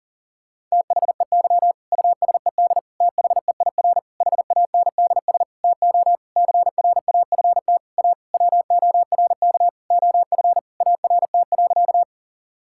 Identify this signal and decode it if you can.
Morse code, heard as a keyed tone, about 40 wpm, 700 Hz. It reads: THEY USED THEIR HANDS TO CRAFT A WORK OF ART.
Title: Sentences from Top 1000 Words - Review Part 6 - 40wpm